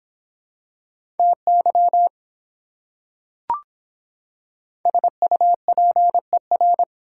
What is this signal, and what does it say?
Morse code, heard as a keyed tone, about 25 wpm, 700 Hz. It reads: TY SUPER